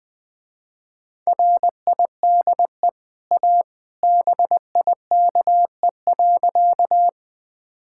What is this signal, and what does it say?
Morse code, heard as a keyed tone, about 21 wpm, 700 Hz. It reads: RIDE A BIKE.
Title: Easy Sentences with Hints - 20wpm